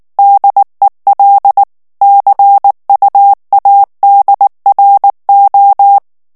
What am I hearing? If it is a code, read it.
Morse code, heard as a keyed tone, about 19 wpm, 800 Hz. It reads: DEL CUADRO
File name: morse mp3 / ESPAGNOL TEXTE EN CLAIR 2 VITESSE 18 MOTS MINUTES